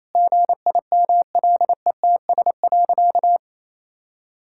Morse code, heard as a keyed tone, about 28 wpm, 700 Hz. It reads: GIMLETS.
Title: Pangram Challenge v2 - 28wpm